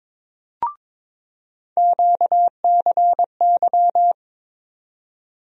Morse code, heard as a keyed tone, about 22 wpm, 700 Hz. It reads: QCY
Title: USA Call Sign Postfixes - 3 Letters 22wpm